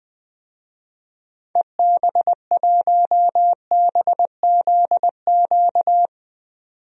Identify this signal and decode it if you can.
Morse code, heard as a keyed tone, about 21 wpm, 700 Hz. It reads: EB1BZQ
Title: DXCC-100 Call Signs - 6 Characters - 20wpm